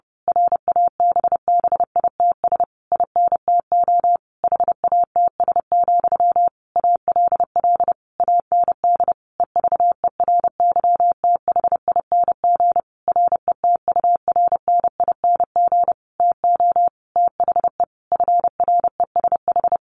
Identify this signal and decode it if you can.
Morse code, heard as a keyed tone, about 30 wpm, 700 Hz. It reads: RABBITS INTO HATS, ALL AND EVERYTHING RETURNING TO THE FRESH
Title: A Sound Of Thunder 30wpm